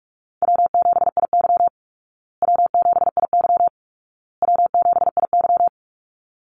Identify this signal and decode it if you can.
Morse code, heard as a keyed tone, about 44 wpm, 700 Hz. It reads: W7IY W7IY W7IY
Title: Call Signs Repeated 2x - 45wpm